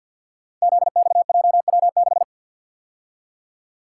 Morse code, heard as a keyed tone, about 53 wpm, 700 Hz. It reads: ZXJW6
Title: Code Groups - 5 Letter:Numbers - 50wpm